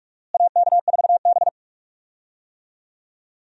Morse code, heard as a keyed tone, about 43 wpm, 700 Hz. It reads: AK4B
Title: USA Call Signs - 4 Characters - Repeated 3x - 45wpm